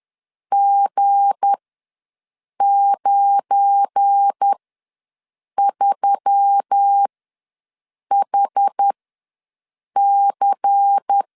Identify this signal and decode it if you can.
Morse code, heard as a keyed tone, about 10 wpm, spaced as for 6 wpm, 800 Hz. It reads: G93HC
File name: LEZIONI / 8 YJ45 / 8 Y ws 10wpm 3952